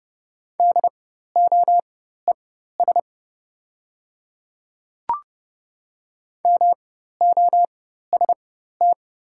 Morse code, heard as a keyed tone, about 30 wpm, spaced as for 14 wpm, 700 Hz. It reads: DOES MOST
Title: 4x Character Spacing - Top 300 Words - 30wpm